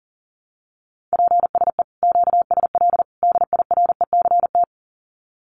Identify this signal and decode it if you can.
Morse code, heard as a keyed tone, about 40 wpm, 700 Hz. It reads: PSE QSL DIRECT
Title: Common QSO Phrases 40wpm